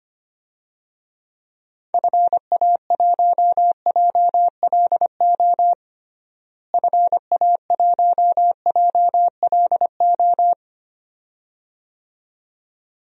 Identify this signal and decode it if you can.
Morse code, heard as a keyed tone, about 25 wpm, 700 Hz. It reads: FA1JLO FA1JLO
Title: DXCC-100 Call Signs - 6 Characters - Repeated 2x - 25wpm